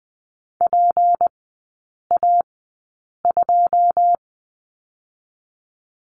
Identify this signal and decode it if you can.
Morse code, heard as a keyed tone, about 20 wpm, 700 Hz. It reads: P A 2
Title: 198 Lesson 32e - Letter k - Three Characters 20wpm